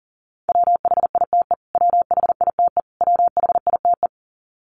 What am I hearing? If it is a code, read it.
Morse code, heard as a keyed tone, about 40 wpm, 700 Hz. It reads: WHITE WHITE WHITE
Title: Top 500 Words Repeated 40wpm